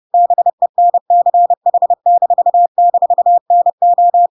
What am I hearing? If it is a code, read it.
Morse code, heard as a keyed tone, about 30 wpm, 700 Hz. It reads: BENCH--NO
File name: Pinocchio / Pinocchio ch1 30 30 ik1hge